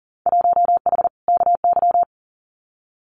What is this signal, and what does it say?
Morse code, heard as a keyed tone, about 42 wpm, 700 Hz. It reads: 1H KY
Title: Winter Field Day Exchanges 40wpm